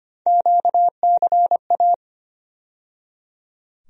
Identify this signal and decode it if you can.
Morse code, heard as a keyed tone, about 25 wpm, 700 Hz. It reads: QCA
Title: USA Call Sign Postfixes - 3 Letters 25wpm